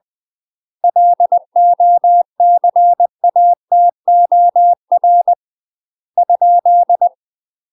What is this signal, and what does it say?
Morse code, heard as a keyed tone, about 20 wpm, 700 Hz. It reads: LOCATOR ?
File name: UK MASTER QSOs TRAINING / 20WPM / UK MASTER QSOs 0072